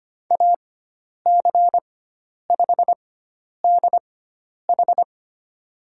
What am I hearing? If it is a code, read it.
Morse code, heard as a keyed tone, about 25 wpm, spaced as for 10 wpm, 700 Hz. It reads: AC5DH
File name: Lesson 16g - Letter c - Call Signs 25wpm